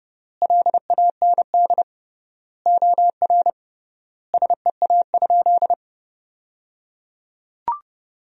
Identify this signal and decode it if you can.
Morse code, heard as a keyed tone, about 29 wpm, 700 Hz. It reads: LAND OR SEA?
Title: 3x Word-Spacing - Sentences from Top 500 Words - 30wpm